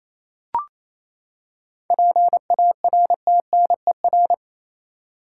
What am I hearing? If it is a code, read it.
Morse code, heard as a keyed tone, about 27 wpm, 700 Hz. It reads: PARTNER
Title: Seven Letter Words - 28wpm